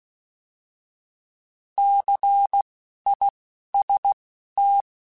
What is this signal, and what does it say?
Morse code, heard as a keyed tone, about 16 wpm, spaced as for 12 wpm, 800 Hz. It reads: CIST
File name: Alphabetical / Alphabetical mp3s / Cs @10